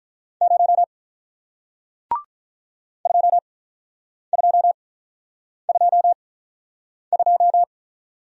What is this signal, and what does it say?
Morse code, heard as a keyed tone, about 46 wpm, spaced as for 9 wpm, 700 Hz. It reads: 0 2222